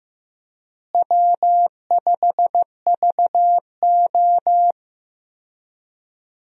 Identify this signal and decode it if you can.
Morse code, heard as a keyed tone, about 15 wpm, 700 Hz. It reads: W5VO
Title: USA Call Signs - 4 Characters - Repeated 2x - 15wpm